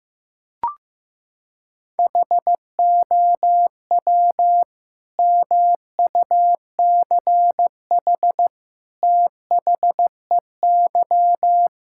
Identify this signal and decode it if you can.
Morse code, heard as a keyed tone, about 15 wpm, 700 Hz. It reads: HOW MUCH THEY
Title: Sentences from Top 1000 Words - Review Part 5 - 15wpm